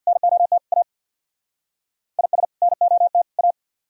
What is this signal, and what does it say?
Morse code, heard as a keyed tone, about 62 wpm, 700 Hz. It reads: NOT A IS NOT A
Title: Speed-Racing - Sets of 3 Words 50wpm